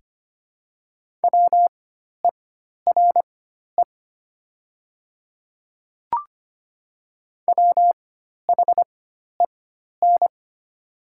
Morse code, heard as a keyed tone, about 25 wpm, spaced as for 12 wpm, 700 Hz. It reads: WERE WHEN
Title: 4x Character Spacing - Top 500 Words - 25wpm